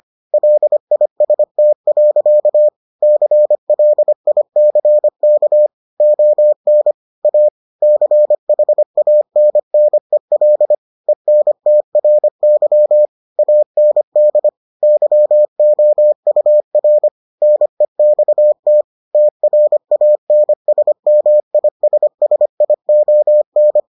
Morse code, heard as a keyed tone, about 25 wpm, 600 Hz. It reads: LIST. CLICK ON A CHANNEL ENTRY AND YOUR NEXT TRANSMISSION